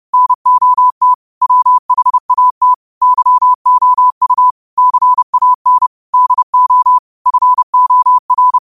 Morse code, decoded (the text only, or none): NOT WHAT YOU CAN DO FOR